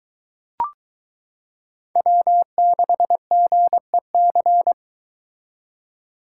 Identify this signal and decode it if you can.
Morse code, heard as a keyed tone, about 22 wpm, 700 Hz. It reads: W6GEC